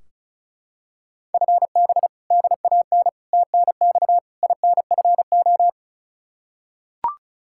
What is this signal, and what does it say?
Morse code, heard as a keyed tone, about 34 wpm, 700 Hz. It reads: FB DAN TNX INFO
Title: Practice QSO - Exchange 3 - More Info - 35wpm